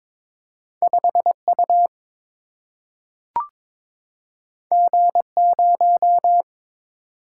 Morse code, heard as a keyed tone, about 22 wpm, 700 Hz. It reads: 5 U G 0
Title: Single-Letter Single-Number 22wpm